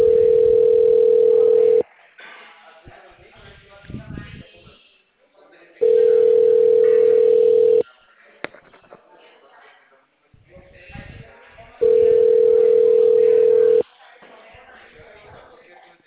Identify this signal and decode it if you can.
Ringback tone (North America), 440 + 480 Hz, with speech.
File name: out- -2016- - - .97694